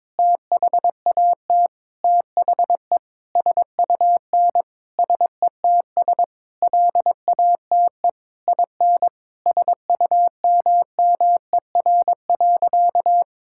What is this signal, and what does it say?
Morse code, heard as a keyed tone, about 21 wpm, 700 Hz. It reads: THAT THE SUN SETS LATE IN SUMMER.
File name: practice / 22wpm / Sentences with 8-Word Phrases - 22wpm